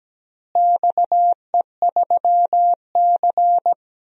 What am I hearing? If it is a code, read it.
Morse code, heard as a keyed tone, about 17 wpm, 700 Hz. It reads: XE3C
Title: DXCC-100 Call Signs - 4 Characters - Repeated 2x - 17wpm